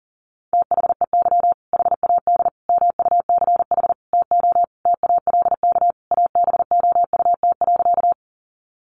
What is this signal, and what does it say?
Morse code, heard as a keyed tone, about 40 wpm, 700 Hz. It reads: THEY HAD MUCH TO TALK ABOUT.
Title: Sentences from Top 400 Words 40wpm